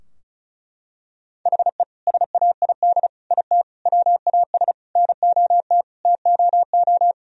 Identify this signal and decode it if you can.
Morse code, heard as a keyed tone, about 34 wpm, 700 Hz. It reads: HE SAID IT WAS NOT TOO
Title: Sentences from Top 1000 Words - Review Part 2 - 35wpm